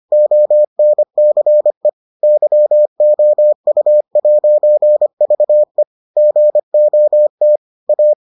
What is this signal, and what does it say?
Morse code, heard as a keyed tone, about 25 wpm, 600 Hz. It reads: ONCE YOU'VE GOT A